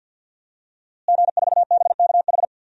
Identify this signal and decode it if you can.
Morse code, heard as a keyed tone, about 52 wpm, 700 Hz. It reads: G4BKH
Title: DXCC-100 Call Signs - 5 Characters - Repeated 3x - 50wpm